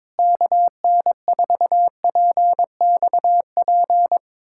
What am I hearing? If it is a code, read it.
Morse code, heard as a keyed tone, about 22 wpm, 700 Hz. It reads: KN4PXP
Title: USA Callsign DE Callsign 22wpm